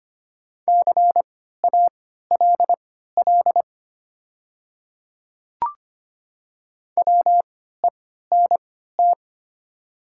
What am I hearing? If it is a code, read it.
Morse code, heard as a keyed tone, about 25 wpm, spaced as for 14 wpm, 690 Hz. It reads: CALL WENT
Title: 3x Character Spacing - Top 200 Words - 25wpm